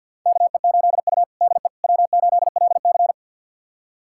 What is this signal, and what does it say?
Morse code, heard as a keyed tone, about 51 wpm, 700 Hz. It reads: KE8U DE W8LC
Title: USA Callsign DE Callsign - 4 Characters - 50wpm